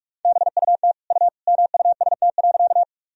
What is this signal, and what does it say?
Morse code, heard as a keyed tone, about 41 wpm, 700 Hz. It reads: BUT U MUST.
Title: Sentences from Top 200 Words and Abbreviations 45wpm